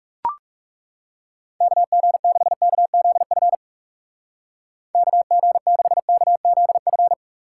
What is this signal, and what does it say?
Morse code, heard as a keyed tone, about 36 wpm, 700 Hz. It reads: KG6KZF KG6KZF
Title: Speed Racing - USA Call Signs 30wpm